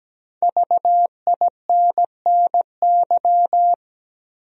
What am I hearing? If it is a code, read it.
Morse code, heard as a keyed tone, about 17 wpm, 700 Hz. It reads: VINNY
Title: USA Common Names - 17wpm